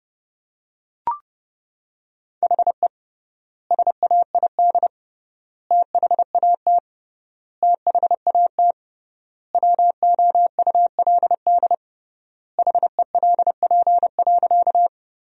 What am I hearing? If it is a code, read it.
Morse code, heard as a keyed tone, about 30 wpm, 700 Hz. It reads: HE SAID THAT THAT WOULD HELP.
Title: 3x Word-Spacing - Sentences from Top 200 Words - 30wpm